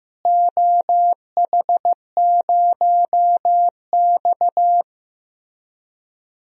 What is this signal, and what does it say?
Morse code, heard as a keyed tone, about 15 wpm, 700 Hz. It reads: OH0X
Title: DXCC-100 Call Signs - 4 Characters - Repeated 3x - 15wpm